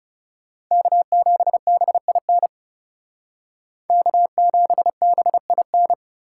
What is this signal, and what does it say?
Morse code, heard as a keyed tone, about 38 wpm, 700 Hz. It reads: K7BIN K7BIN